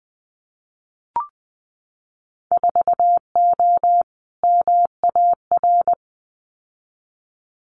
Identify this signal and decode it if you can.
Morse code, heard as a keyed tone, about 20 wpm, 700 Hz. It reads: 4O MAR
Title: Winter Field Day Exchanges Encore 20wpm